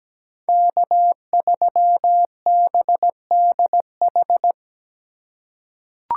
Morse code, heard as a keyed tone, about 17 wpm, 700 Hz. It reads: K3BDH
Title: USA Call Signs - 5 Characters - 4x Character Spacing - 17wpm